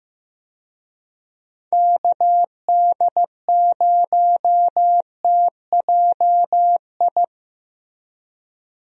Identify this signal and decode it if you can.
Morse code, heard as a keyed tone, about 15 wpm, 700 Hz. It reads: KD0TJI
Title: USA Call Signs - 6 Characters - Repeated 3x - 15wpm